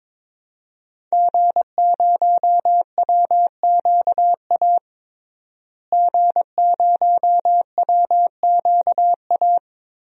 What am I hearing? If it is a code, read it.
Morse code, heard as a keyed tone, about 22 wpm, 700 Hz. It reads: G0WQA G0WQA
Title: DXCC-100 Call Signs - 5 Characters - Repeated 2x - 22wpm